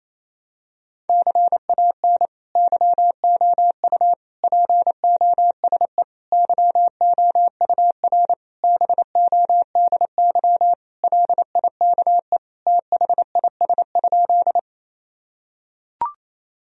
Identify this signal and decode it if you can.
Morse code, heard as a keyed tone, about 28 wpm, 700 Hz. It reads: CAN YOU POSE YOUR BODY LIKE THIS?
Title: Sentences from Top 1000 Words - Review Part 2 - 28wpm